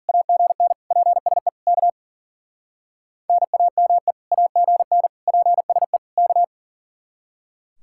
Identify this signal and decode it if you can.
Morse code, heard as a keyed tone, about 51 wpm, 700 Hz. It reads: AGN PSE K NAME AGN PSE K